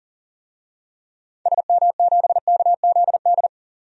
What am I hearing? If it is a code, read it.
Morse code, heard as a keyed tone, about 40 wpm, 700 Hz. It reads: SM7KZD
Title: DXCC-100 Call Signs - 6 Characters - 40wpm